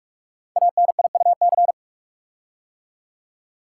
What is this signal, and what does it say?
Morse code, heard as a keyed tone, about 44 wpm, 700 Hz. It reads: ANIUC
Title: Code Groups - 5 Letter:Numbers - 45wpm